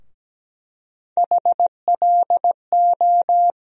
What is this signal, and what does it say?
Morse code, heard as a keyed tone, about 17 wpm, 700 Hz. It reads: HLO
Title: USA Call Sign Postfixes - 3 Letters 17wpm